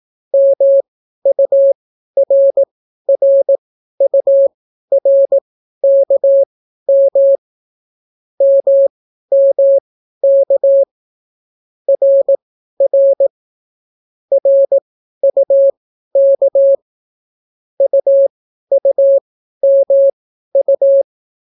Morse code, heard as a keyed tone, about 18 wpm, spaced as for 12 wpm, 550 Hz. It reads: MURRURKM MMK RR RUK UUMU